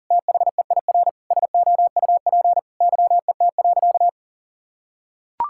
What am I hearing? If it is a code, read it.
Morse code, heard as a keyed tone, about 40 wpm, 700 Hz. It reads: THEIR SOUP YET.
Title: Sentences from Top 1000 Words - Review Part 6 - 40wpm